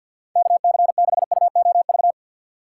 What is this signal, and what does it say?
Morse code, heard as a keyed tone, about 53 wpm, 700 Hz. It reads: KC6UOV
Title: USA Call Signs - 3x Character Spacing - 50wpm